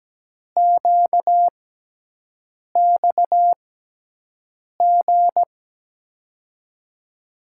Morse code, heard as a keyed tone, about 17 wpm, spaced as for 6 wpm, 700 Hz. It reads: QXG
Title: USA Call Signs - 5 Characters - 6x Character Spacing - 17wpm